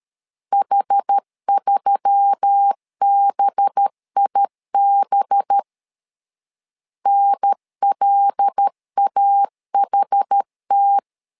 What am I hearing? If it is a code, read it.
Morse code, heard as a keyed tone, about 13 wpm, 800 Hz. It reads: H3BIB NLAHT
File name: LEZIONI / 5 LBVC / 5 LB 12wpm 1906